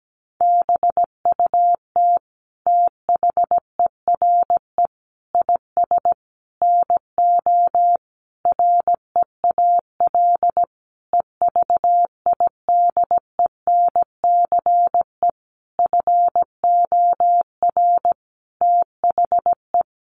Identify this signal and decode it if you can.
Morse code, heard as a keyed tone, about 17 wpm, 700 Hz. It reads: BUT THERE IS NO REAL EVIDENCE FOR THE
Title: Sentences from Top 1300 Words 17wpm